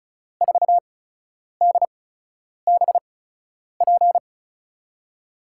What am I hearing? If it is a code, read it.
Morse code, heard as a keyed tone, about 36 wpm, spaced as for 10 wpm, 700 Hz. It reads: 4DBP
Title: USA Call Signs - 6 Characters - 8x Character Spacing - 35wpm